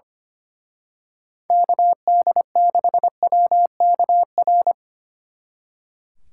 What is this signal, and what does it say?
Morse code, heard as a keyed tone, about 28 wpm, 700 Hz. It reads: KD6WKR
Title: Speed Racing - USA 2x3 Call Signs 25wpm